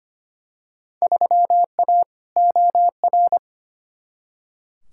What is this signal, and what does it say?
Morse code, heard as a keyed tone, about 25 wpm, 700 Hz. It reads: 3A OR
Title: Field Day Exchanges - 25wpm